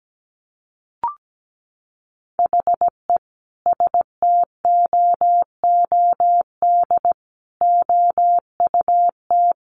Morse code, heard as a keyed tone, about 17 wpm, 700 Hz. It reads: HE STOOD OUT
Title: Sentences from Top 1500 Words 17wpm